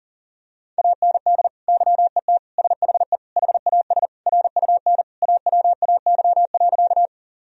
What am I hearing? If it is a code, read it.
Morse code, heard as a keyed tone, about 40 wpm, 700 Hz. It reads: AND YET SHE HAS RUN AWAY.
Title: Sentences from Top 1000 Words - Review Part 1 - 40wpm